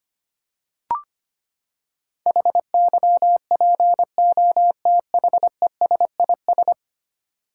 Morse code, heard as a keyed tone, about 25 wpm, 700 Hz. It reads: HYPOTHESIS
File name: Ten Letter Words - 25wpm